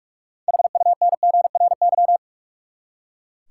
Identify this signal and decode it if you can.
Morse code, heard as a keyed tone, about 42 wpm, 700 Hz. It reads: HUNGRY
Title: Six Letter Words - 45wpm